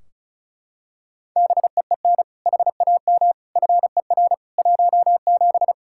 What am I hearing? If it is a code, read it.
Morse code, heard as a keyed tone, about 36 wpm, 700 Hz. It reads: BEEN HAM FER 17
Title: Practice QSO - Exchange 3 - More Info - 35wpm